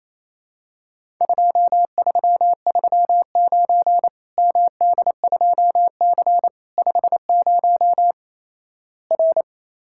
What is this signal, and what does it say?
Morse code, heard as a keyed tone, about 28 wpm, 700 Hz. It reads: 2339 MD2C 50 R
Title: Worked All Europe DX Contest - QTC Exchange - 28wpm